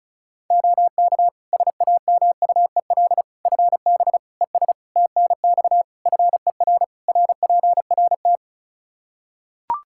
Morse code, heard as a keyed tone, about 36 wpm, 700 Hz. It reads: OK SAMUEL FB ES TNX FER RPRT
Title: Practice QSO - Exchange 2 - Info - 35wpm